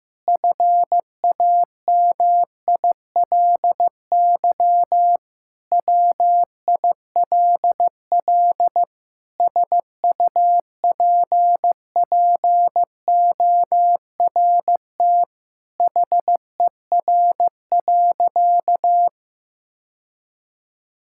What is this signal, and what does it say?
Morse code, heard as a keyed tone, about 15 wpm, 700 Hz. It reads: FAMILY WILL SUPPORT HER.
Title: Sentences from Top 1000 Words - Review Part 5 - 15wpm